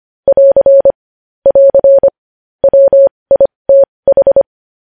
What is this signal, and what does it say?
Morse code, heard as a keyed tone, about 25 wpm, spaced as for 20 wpm, 550 Hz. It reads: <AR> <AR> WITH
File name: podcast / News headlines-1x-25x20